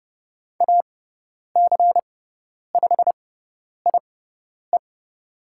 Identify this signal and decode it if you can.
Morse code, heard as a keyed tone, about 30 wpm, spaced as for 10 wpm, 700 Hz. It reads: AC5IE
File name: Lesson 17g - Letter u - Call Signs 30wpm